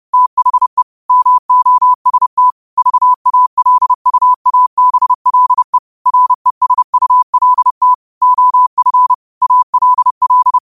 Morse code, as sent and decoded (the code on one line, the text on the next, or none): - .... / -- --- ... - / ...- .- .-.. ..- .- -... .-.. / .-. ... ..- .-.. - / --- ..-. / .- .-.. .-..
THE MOST VALUABLE RESULT OF ALL